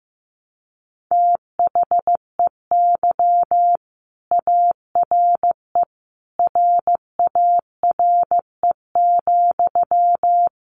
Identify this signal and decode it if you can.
Morse code, heard as a keyed tone, about 15 wpm, 700 Hz. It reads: THEY ARE RARE,
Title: Sentences from Top 1100 Words 15wpm